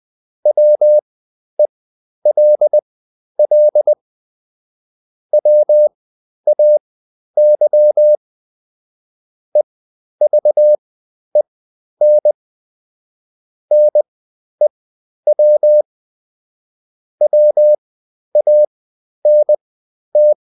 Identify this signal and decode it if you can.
Morse code, heard as a keyed tone, about 20 wpm, spaced as for 11 wpm, 600 Hz. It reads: WELL WAY EVEN NEW WANT
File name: Top-100-Words-10